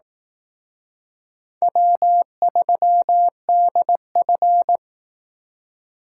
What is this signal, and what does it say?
Morse code, heard as a keyed tone, about 18 wpm, 700 Hz. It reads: W3DF
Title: Speed Racing - USA 1x2 Call Signs 15wpm